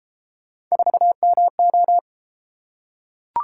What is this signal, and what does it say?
Morse code, heard as a keyed tone, about 30 wpm, 700 Hz. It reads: 4MO